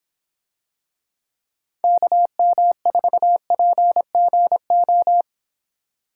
Morse code, heard as a keyed tone, about 26 wpm, 700 Hz. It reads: KM4PGO